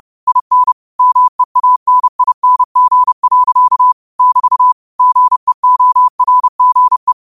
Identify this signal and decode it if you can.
Morse code, heard as a keyed tone, about 30 wpm, 1000 Hz. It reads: IN MEANING. <BT> GEORGE
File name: qotdfiles / 30wpm